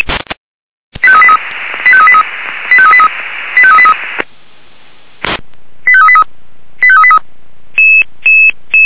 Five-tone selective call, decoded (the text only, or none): 85383 85383 85383 85383 85383 85383